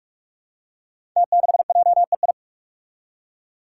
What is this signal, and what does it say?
Morse code, heard as a keyed tone, about 44 wpm, 700 Hz. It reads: T6JEI
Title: Five Challenging Letter-Number Combinations 45wpm